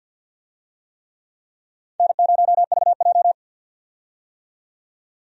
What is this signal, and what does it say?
Morse code, heard as a keyed tone, about 51 wpm, 700 Hz. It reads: N0VJ
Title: USA Call Signs - 4 Characters - 50wpm